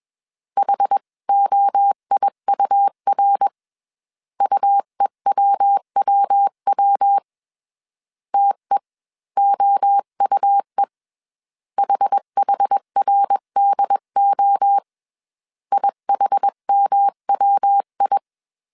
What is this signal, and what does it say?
Morse code, heard as a keyed tone, about 21 wpm, 800 Hz. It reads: HOIUR UEWWW TEOUE HHRDO IHMWI